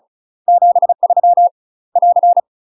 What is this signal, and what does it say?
Morse code, heard as a keyed tone, about 36 wpm, 700 Hz. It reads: 73 <AR>
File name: UK MASTER QSOs TRAINING / 35WPM / UK MASTER QSOs 0026